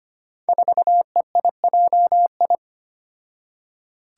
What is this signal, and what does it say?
Morse code, heard as a keyed tone, about 25 wpm, 700 Hz. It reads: 4EIJI
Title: Five Challenging Letter-Number Combinations 25wpm